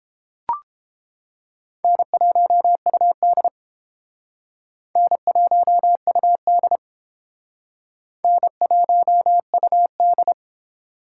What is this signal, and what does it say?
Morse code, heard as a keyed tone, about 31 wpm, 700 Hz. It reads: N1UD N1UD N1UD